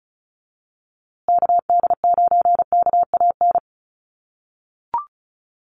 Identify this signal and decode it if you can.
Morse code, heard as a keyed tone, about 34 wpm, 700 Hz. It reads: KD9KAN
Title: USA Call Signs Repeated 3x 35wpm